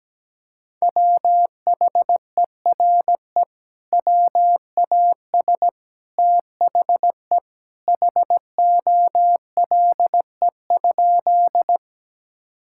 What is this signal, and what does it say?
Morse code, heard as a keyed tone, about 17 wpm, 700 Hz. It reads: WHERE WAS THE HOLE?
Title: Sentences from Top 1600 Words 17wpm